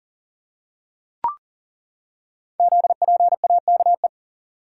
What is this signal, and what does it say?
Morse code, heard as a keyed tone, about 38 wpm, 690 Hz. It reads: ZPAKE
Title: Code Groups - 5 Letter:Numbers - 40wpm